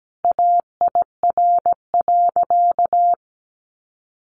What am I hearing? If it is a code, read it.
Morse code, heard as a keyed tone, about 17 wpm, 700 Hz. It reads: AIR.
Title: Sentences from Top 1200 Words 17wpm